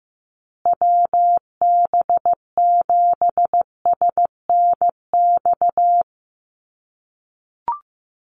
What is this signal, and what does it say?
Morse code, heard as a keyed tone, about 15 wpm, 700 Hz. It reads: WB7SNX
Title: USA Call Signs Repeated 3x 15wpm